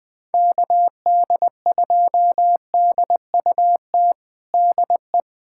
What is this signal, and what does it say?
Morse code, heard as a keyed tone, about 20 wpm, 700 Hz. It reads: KD2DUT DE
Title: USA Callsign DE Callsign - 6 Characters - 20wpm